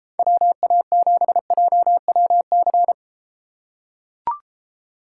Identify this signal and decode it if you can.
Morse code, heard as a keyed tone, about 32 wpm, 700 Hz. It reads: WA7JWC